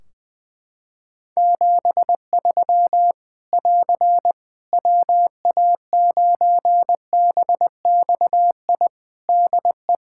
Morse code, heard as a keyed tone, about 21 wpm, 700 Hz. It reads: 73 <AR> WA9BXI DE
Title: K1USN SST QSO Practice 20wpm